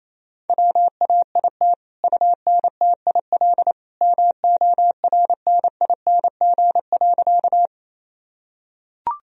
Morse code, heard as a keyed tone, about 28 wpm, 700 Hz. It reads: WAIT UNTIL MORNING.
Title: Sentences from Top 1000 Words - Review Part 1 - 28wpm